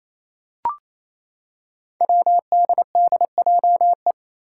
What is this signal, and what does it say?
Morse code, heard as a keyed tone, about 28 wpm, 700 Hz. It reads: WDDJE
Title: Five Challenging Letter-Number Combinations 28wpm